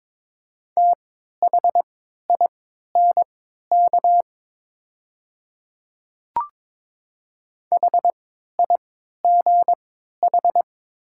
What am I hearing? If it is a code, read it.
Morse code, heard as a keyed tone, about 23 wpm, spaced as for 13 wpm, 700 Hz. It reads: THINK HIGH
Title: 3x Character Spacing - Top 300 Words - 22wpm